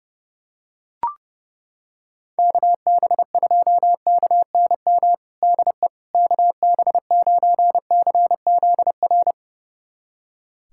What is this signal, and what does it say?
Morse code, heard as a keyed tone, about 32 wpm, 700 Hz. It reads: KB2KNM DE KB9CZR
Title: USA Callsign DE Callsign - 6 Characters - 30wpm